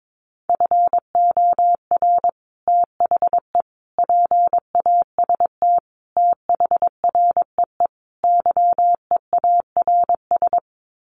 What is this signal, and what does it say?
Morse code, heard as a keyed tone, about 22 wpm, 700 Hz. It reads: FOR THE PAST THREE YEARS
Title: Sets of 5 Words Encore2 22wpm